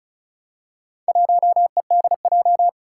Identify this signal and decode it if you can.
Morse code, heard as a keyed tone, about 36 wpm, 700 Hz. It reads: 1EDJ
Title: Four Challenging Letter-Number Combinations 35wpm